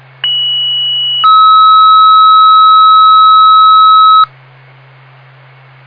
Two-tone page, heard as a one-second tone, then a three-second tone, about 2950 then 1290 Hz. It is heard from 0.2 to 4.2 s.